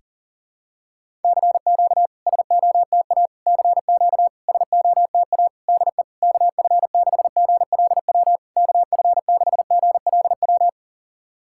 Morse code, heard as a keyed tone, about 40 wpm, 690 Hz. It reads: CQ SOTA CQ SOTA DE KF6GLW KF6GLW
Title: SOTA QSO Practice 40wpm